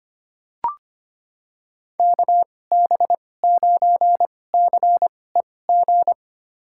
Morse code, heard as a keyed tone, about 25 wpm, spaced as for 18 wpm, 700 Hz. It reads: KB9CEG
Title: USA Call Signs - 6 Characters - 2x Character Spacing - 25wpm